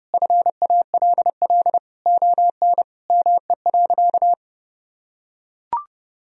Morse code, heard as a keyed tone, about 30 wpm, 700 Hz. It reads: FALL ON ME.
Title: Sentences from Top 1000 Words - Review Part 2 - 30wpm